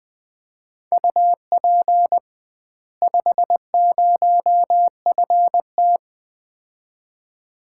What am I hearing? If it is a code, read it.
Morse code, heard as a keyed tone, about 20 wpm, 700 Hz. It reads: UP 50FT
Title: Common QSO Phrases - 2x Word-Spacing - 20wpm